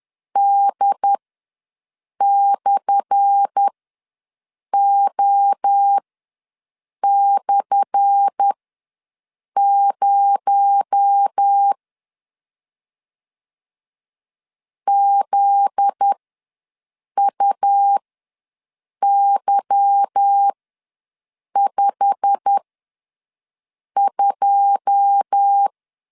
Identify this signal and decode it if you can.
Morse code, heard as a keyed tone, about 10 wpm, spaced as for 6 wpm, 800 Hz. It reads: D/O/0 ZUY52
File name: LEZIONI / 10 SIGNS / 10A SLASH ws 10wpm 4121